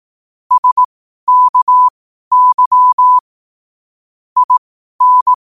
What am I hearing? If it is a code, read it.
Morse code, heard as a keyed tone, about 18 wpm, spaced as for 12 wpm, 1000 Hz. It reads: SKY IN
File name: qotdfiles / 12wpm